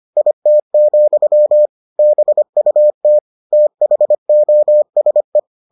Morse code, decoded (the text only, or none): IT, BUT THOSE